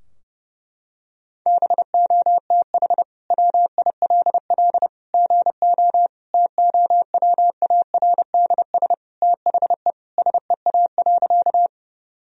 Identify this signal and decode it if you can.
Morse code, heard as a keyed tone, about 30 wpm, 700 Hz. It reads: BOTH WILL GO TOWARDS THE SEA.
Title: Sentences from Top 1000 Words - Review Part 2 - 30wpm